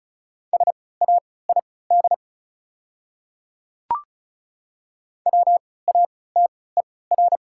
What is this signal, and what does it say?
Morse code, heard as a keyed tone, about 34 wpm, spaced as for 20 wpm, 690 Hz. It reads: SAID WATER
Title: 3x Character Spacing - Top 100 Words - 35wpm